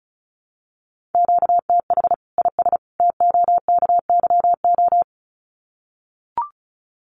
Morse code, heard as a keyed tone, about 34 wpm, 700 Hz. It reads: QTH IS TOKYO
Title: Common QSO Phrases 35wpm